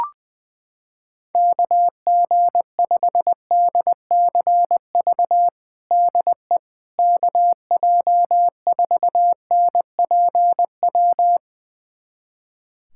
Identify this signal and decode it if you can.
Morse code, heard as a keyed tone, about 20 wpm, 700 Hz. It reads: KG5DCV DE KJ4NPW
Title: USA Callsign DE Callsign - 6 Characters - 20wpm